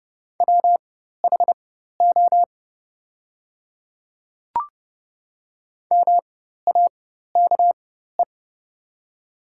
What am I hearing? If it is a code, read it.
Morse code, heard as a keyed tone, about 30 wpm, spaced as for 14 wpm, 690 Hz. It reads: WHO MAKE
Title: 4x Character Spacing - Top 100 Words - 30wpm